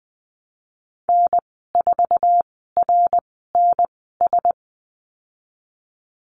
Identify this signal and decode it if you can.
Morse code, heard as a keyed tone, about 19 wpm, spaced as for 14 wpm, 700 Hz. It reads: N4RNS
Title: 076 Lesson 13g - Letter L - Call Signs 20wpm